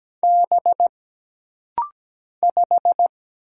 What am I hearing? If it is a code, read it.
Morse code, heard as a keyed tone, about 17 wpm, 700 Hz. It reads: B 5
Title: Single Letter-Number - Rapid-Fire 17wpm